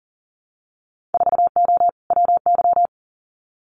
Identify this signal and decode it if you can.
Morse code, heard as a keyed tone, about 40 wpm, 700 Hz. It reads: 4O WY
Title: Winter Field Day Exchanges Encore 40wpm